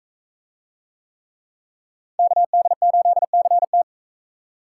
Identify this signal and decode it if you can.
Morse code, heard as a keyed tone, about 40 wpm, 700 Hz. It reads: KD8CT